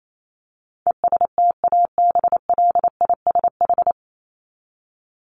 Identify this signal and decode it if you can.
Morse code, heard as a keyed tone, about 28 wpm, 700 Hz. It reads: ESTABLISH
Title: ED Suffix - Top 100 Words with Base Words 28wpm